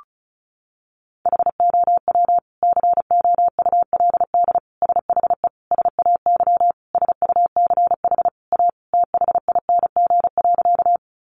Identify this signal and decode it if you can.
Morse code, heard as a keyed tone, about 34 wpm, 700 Hz. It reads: HOW COULD SHE SAY SUCH A THING.
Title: Sentences from Top 200 Words 35wpm